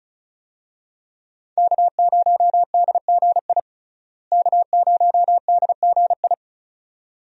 Morse code, heard as a keyed tone, about 34 wpm, 700 Hz. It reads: K0DGI K0DGI
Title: USA Call Signs - 5 Characters - Repeated 2x - 35wpm